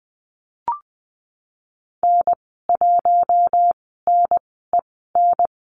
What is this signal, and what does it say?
Morse code, heard as a keyed tone, about 20 wpm, spaced as for 15 wpm, 700 Hz. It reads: N1NEN
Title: 048 Lesson 9g - Number 4 - Call Signs 20wpm